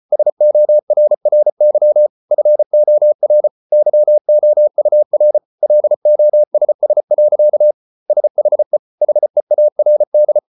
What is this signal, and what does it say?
Morse code, heard as a keyed tone, about 34 wpm, 600 Hz. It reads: SORRY FOR YOUR LOSS. SHE HEARD